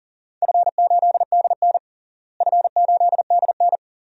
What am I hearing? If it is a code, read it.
Morse code, heard as a keyed tone, about 40 wpm, 700 Hz. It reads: F8DN F8DN
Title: DXCC-100 Call Signs - 4 Characters - Repeated 2x - 40wpm